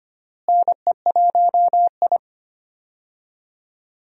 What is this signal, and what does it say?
Morse code, heard as a keyed tone, about 25 wpm, 700 Hz. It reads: NE1I
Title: USA Call Signs - 4 Characters - Repeated 2x - 25wpm